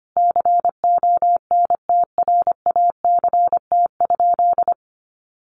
Morse code, heard as a keyed tone, about 25 wpm, 700 Hz. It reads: CONTRACT?
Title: Sentences from Top 1400 Words 25wpm